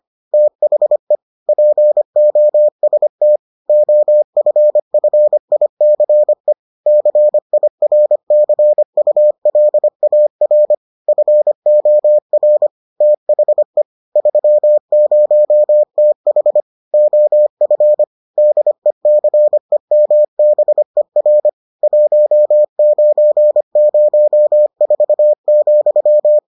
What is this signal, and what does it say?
Morse code, heard as a keyed tone, about 25 wpm, 600 Hz. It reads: THE POST OFFICE CIRCULAR FOR THE 30TH OF DECEMBER 1904,